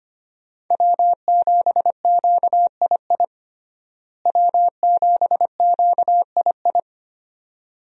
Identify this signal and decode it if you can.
Morse code, heard as a keyed tone, about 25 wpm, 700 Hz. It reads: W7QII W7QII
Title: USA Call Signs - 5 Characters - Repeated 2x - 25wpm